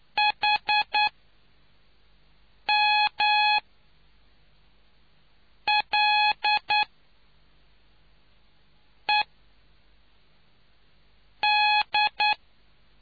Morse code, heard as a keyed tone, about 9 wpm, spaced as for 4 wpm, 840 Hz. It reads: HMLED